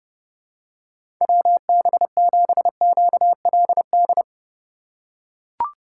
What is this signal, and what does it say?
Morse code, heard as a keyed tone, about 30 wpm, 700 Hz. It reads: WB7QLD
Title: USA Call Signs - 4x Character Spacing - 30wpm